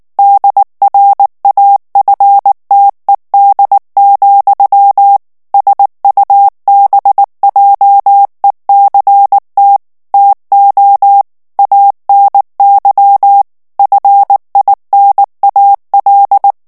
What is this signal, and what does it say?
Morse code, heard as a keyed tone, about 19 wpm, 800 Hz. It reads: DRAFTED, SUBJECT TO ANY FINAL
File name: ANGLAIS texte clair 3 vitesse 18 mots minute